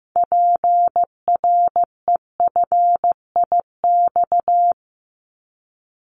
Morse code, heard as a keyed tone, about 15 wpm, 700 Hz. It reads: PREFIX
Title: Callsign Prefixes for Top 100 DXCC - 15wpm